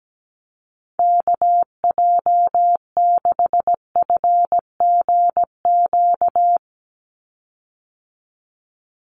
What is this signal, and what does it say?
Morse code, heard as a keyed tone, about 17 wpm, 700 Hz. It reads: KJ6FGQ
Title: USA Call Signs 17wpm